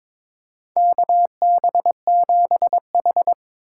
Morse code, heard as a keyed tone, about 21 wpm, 700 Hz. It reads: KB7H
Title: K1USN SST QSO Practice - 22wpm